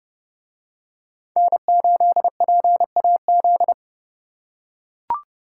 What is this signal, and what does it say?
Morse code, heard as a keyed tone, about 29 wpm, 700 Hz. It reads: N8PAZ